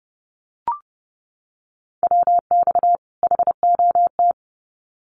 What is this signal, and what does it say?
Morse code, heard as a keyed tone, about 30 wpm, 700 Hz. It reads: WX HOT
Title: Common QSO Phrases 30wpm